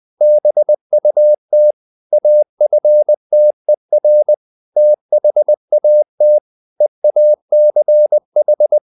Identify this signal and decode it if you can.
Morse code, heard as a keyed tone, about 20 wpm, 600 Hz. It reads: BUT AFTER THAT EACH
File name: the old man and the sea part 2 20wpm